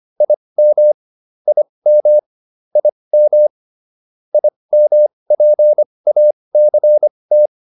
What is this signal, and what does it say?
Morse code, heard as a keyed tone, about 25 wpm, spaced as for 20 wpm, 600 Hz. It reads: IM IM IM IMPACT